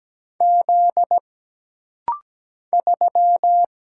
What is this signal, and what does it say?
Morse code, heard as a keyed tone, about 17 wpm, 700 Hz. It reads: Z 3
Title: Single Letter-Number Alt - Rapid-Fire 17wpm